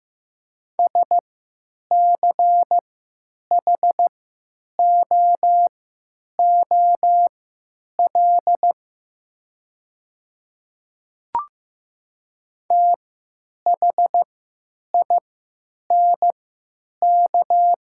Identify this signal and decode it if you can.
Morse code, heard as a keyed tone, about 15 wpm, spaced as for 9 wpm, 700 Hz. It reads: SCHOOL THINK
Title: 3x Character Spacing - Top 500 Words - 15wpm